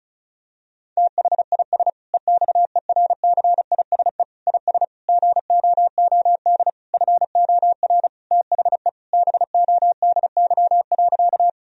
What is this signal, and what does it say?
Morse code, heard as a keyed tone, about 36 wpm, 700 Hz. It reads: THIS EXERCISE IS GOOD FOR THE BODY.
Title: Sentences from Top 1000 Words - Review Part 4 - 35wpm